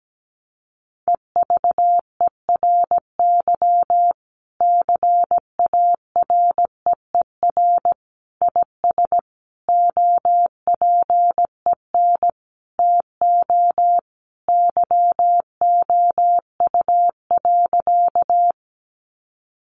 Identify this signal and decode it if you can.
Morse code, heard as a keyed tone, about 17 wpm, 700 Hz. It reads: EVERY CAREER IS OPEN TO YOU.
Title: Sentences from Top 1000 Words 17wpm